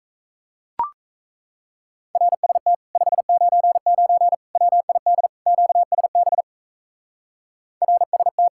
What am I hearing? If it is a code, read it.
Morse code, heard as a keyed tone, about 34 wpm, 700 Hz. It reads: RST 599 WID QSB RST
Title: Speed-Racing Common QSO Phrases - 28wpm